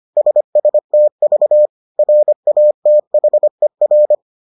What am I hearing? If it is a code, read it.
Morse code, heard as a keyed tone, about 25 wpm, 600 Hz. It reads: SSTV RATHER